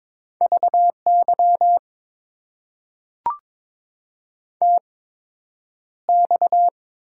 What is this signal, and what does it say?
Morse code, heard as a keyed tone, about 22 wpm, spaced as for 9 wpm, 700 Hz. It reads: VY TX
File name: Top 90 QSO Elements - 8x Character Spacing - 22wpm